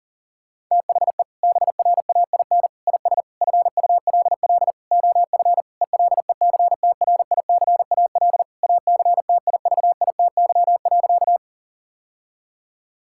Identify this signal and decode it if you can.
Morse code, heard as a keyed tone, about 40 wpm, 700 Hz. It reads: THE BRAIN IS FULL OF ELECTRICAL ACTIVITY.
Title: Sentences from Top 1000 Words - Review Part 6 - 40wpm